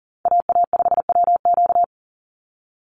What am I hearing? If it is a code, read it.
Morse code, heard as a keyed tone, about 40 wpm, 700 Hz. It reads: AA5WQ
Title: USA Call Signs 40wpm